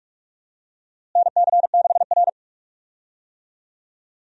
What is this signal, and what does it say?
Morse code, heard as a keyed tone, about 46 wpm, 700 Hz. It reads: NC6R
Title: USA Call Signs - 4 Characters - Repeated 3x - 45wpm